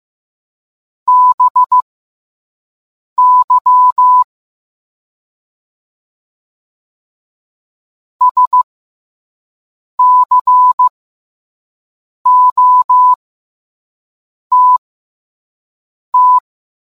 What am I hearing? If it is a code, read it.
Morse code, heard as a keyed tone, about 15 wpm, spaced as for 5 wpm, 1000 Hz. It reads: BY SCOTT